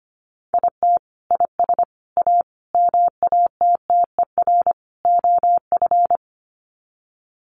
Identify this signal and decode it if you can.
Morse code, heard as a keyed tone, about 25 wpm, 700 Hz. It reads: IT IS A MATTER OF
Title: Sets of 5 Words Encore2 25wpm